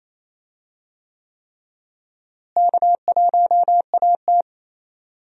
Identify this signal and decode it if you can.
Morse code, heard as a keyed tone, about 27 wpm, 700 Hz. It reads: K1AT
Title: USA Call Signs - 4 Characters - Repeated 3x - 28wpm